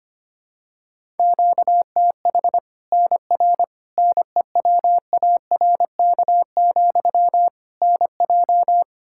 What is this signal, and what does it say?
Morse code, heard as a keyed tone, about 25 wpm, 700 Hz. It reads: QTH NR NEWARK, NJ
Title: Practice QSO - Exchange 1 - Beginning - 25wpm